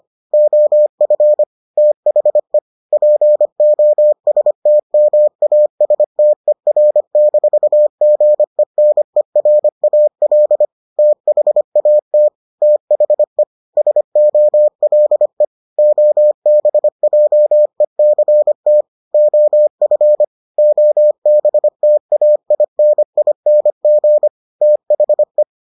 Morse code, heard as a keyed tone, about 25 wpm, 600 Hz. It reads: OF THE POSTMASTER-GENERAL THAT THE SOLE OBJECT OF OBTAINING THE